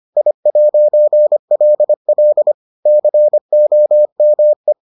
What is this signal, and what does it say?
Morse code, heard as a keyed tone, about 25 wpm, 600 Hz. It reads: I'LL COME